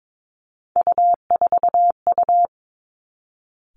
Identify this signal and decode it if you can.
Morse code, heard as a keyed tone, about 22 wpm, 700 Hz. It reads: U4U
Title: Three Challenging Letter-Number Combinations 22wpm